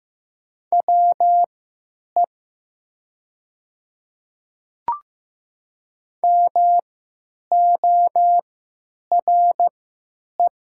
Morse code, heard as a keyed tone, about 15 wpm, spaced as for 9 wpm, 700 Hz. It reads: WE MORE
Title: 3x Character Spacing - Top 100 Words - 15wpm